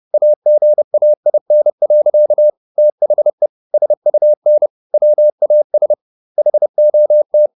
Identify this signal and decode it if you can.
Morse code, heard as a keyed tone, about 30 wpm, 600 Hz. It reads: AGAIN. THE SUN WAS HOT
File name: the old man and the sea / the old man and the sea 1 30wpm